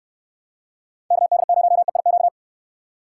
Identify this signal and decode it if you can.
Morse code, heard as a keyed tone, about 64 wpm, 700 Hz. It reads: KD0EIQ